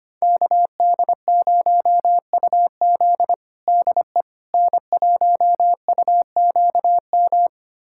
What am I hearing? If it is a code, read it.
Morse code, heard as a keyed tone, about 25 wpm, 700 Hz. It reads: KD0UZ DE N1UQM
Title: USA Callsign DE Callsign - 5 Characters - 25wpm